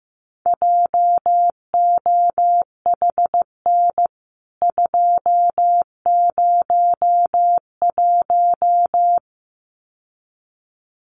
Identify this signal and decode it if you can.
Morse code, heard as a keyed tone, about 15 wpm, 700 Hz. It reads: JOHN 201
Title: CWT Exchange 15wpm